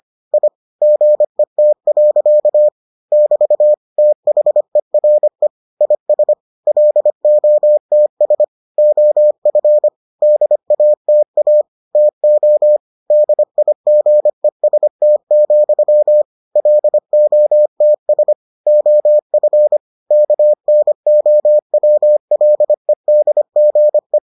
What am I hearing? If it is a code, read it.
Morse code, heard as a keyed tone, about 25 wpm, 600 Hz. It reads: I GET. <BT> THERE IS LOTS OF DATA TO DIGEST, LOTS OF KNOWLEDGE